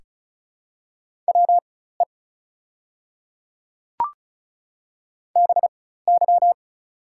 Morse code, heard as a keyed tone, about 35 wpm, spaced as for 16 wpm, 690 Hz. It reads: WE BY
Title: 4x Character Spacing - Top 100 Words - 35wpm